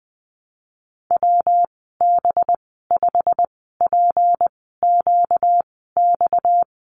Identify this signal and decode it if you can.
Morse code, heard as a keyed tone, about 20 wpm, spaced as for 15 wpm, 700 Hz. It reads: WB5PQX
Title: 238 Lesson 38g - Letter q - Call Signs 20wpm